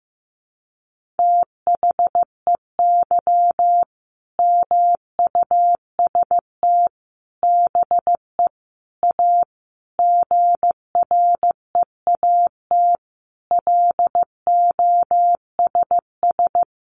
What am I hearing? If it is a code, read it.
Morse code, heard as a keyed tone, about 15 wpm, 700 Hz. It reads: THEY MUST BE A GREAT LOSS
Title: Sentences from Top 1400 Words 15wpm